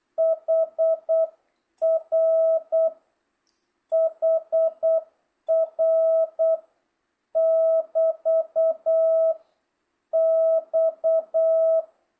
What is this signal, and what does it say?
Morse code, heard as a keyed tone, about 8 wpm, 1300 Hz, with noise. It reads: HR HR <BT> X